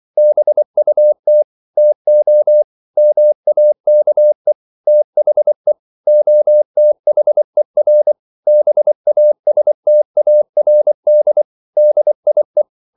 Morse code, heard as a keyed tone, about 24 wpm, 600 Hz. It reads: BUT TO MAKE THE OTHER BASTARD DIE
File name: CW / Practice / EnglishQuotations0324